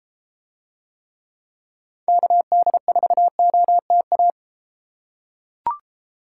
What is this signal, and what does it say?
Morse code, heard as a keyed tone, about 32 wpm, 700 Hz. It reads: KD4OTA